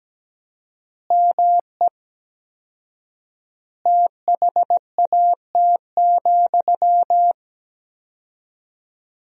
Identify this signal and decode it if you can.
Morse code, heard as a keyed tone, about 17 wpm, 700 Hz. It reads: ME THAT,
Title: 4x Word-Spacing - Sentences from Top 500 Words - 17wpm